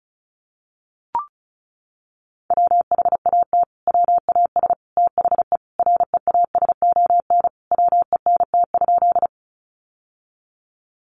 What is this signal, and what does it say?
Morse code, heard as a keyed tone, about 34 wpm, 700 Hz. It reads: WHAT WAS THE REASON WENT?
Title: Sentences from Top 700 Words 35wpm